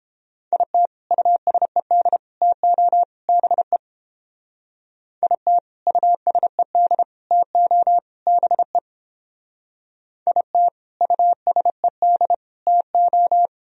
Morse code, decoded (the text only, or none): IT USED TO BE IT USED TO BE IT USED TO